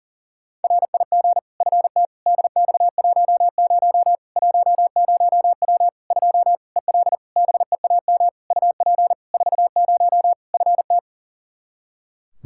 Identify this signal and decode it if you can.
Morse code, heard as a keyed tone, about 40 wpm, 700 Hz. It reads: RIG FT DX10 10W 2 EL BEAM UP 40 FT
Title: Practice QSO - Exchange 2 - Info - 40wpm